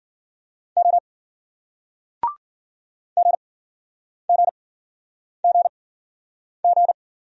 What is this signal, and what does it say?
Morse code, heard as a keyed tone, about 55 wpm, spaced as for 9 wpm, 700 Hz. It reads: O GGGG